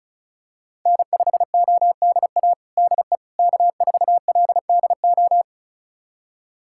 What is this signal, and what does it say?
Morse code, heard as a keyed tone, about 34 wpm, 700 Hz. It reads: N5ODA DE K4LDO
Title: USA Callsign DE Callsign - 5 Characters - 35wpm